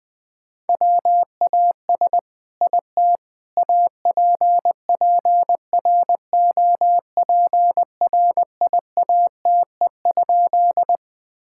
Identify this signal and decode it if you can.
Morse code, heard as a keyed tone, about 21 wpm, 700 Hz. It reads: WAS IT APPROPRIATE?
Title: Sentences from Top 1000 Words - Review Part 5 - 20wpm